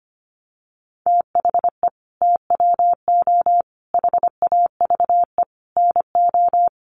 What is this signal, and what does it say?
Morse code, heard as a keyed tone, about 25 wpm, 700 Hz. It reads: THE TWO HAVE NO
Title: Sentences from Top 1300 Words 25wpm